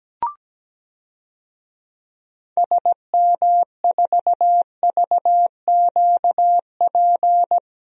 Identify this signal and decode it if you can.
Morse code, heard as a keyed tone, about 17 wpm, 700 Hz. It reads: SM4VQP
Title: DXCC-100 Call Signs - 6 Characters - 17wpm